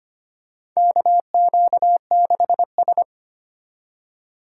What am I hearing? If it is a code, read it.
Morse code, heard as a keyed tone, about 25 wpm, 700 Hz. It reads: KQ6S
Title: USA Call Signs - 4 Characters - 2x Character Spacing - 25wpm